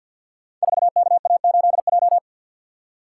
Morse code, heard as a keyed tone, about 49 wpm, 700 Hz. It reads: 4KA8J
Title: Code Groups - 5 Letter:Numbers - 50wpm